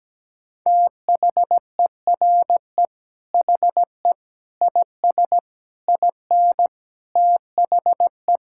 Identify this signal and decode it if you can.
Morse code, heard as a keyed tone, about 17 wpm, 700 Hz. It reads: THERE HE IS IN THE
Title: Sentences from Top 1600 Words 17wpm